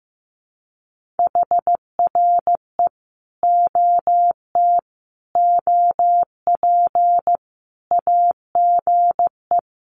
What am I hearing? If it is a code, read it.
Morse code, heard as a keyed tone, about 15 wpm, 700 Hz. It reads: HRE OT OP AGE
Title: Common QSO Phrases 15wpm